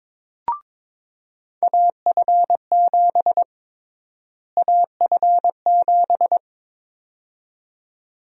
Morse code, heard as a keyed tone, about 22 wpm, 700 Hz. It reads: AF7 AF7
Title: USA Call Sign Prefixes - Repeated 2x - 22wpm